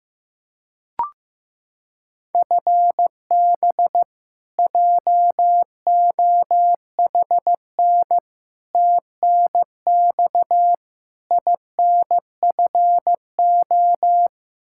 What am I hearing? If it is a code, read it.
Morse code, heard as a keyed tone, about 15 wpm, 700 Hz. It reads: FB JOHN TNX INFO
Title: Practice QSO - Exchange 3 - More Info - 15wpm